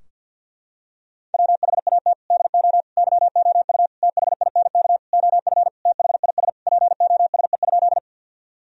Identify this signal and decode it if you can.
Morse code, heard as a keyed tone, about 51 wpm, 700 Hz. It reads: WHAT DO YOU THINK OF THIS POSE?
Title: Sentences from Top 1000 Words - Review Part 2 - 50wpm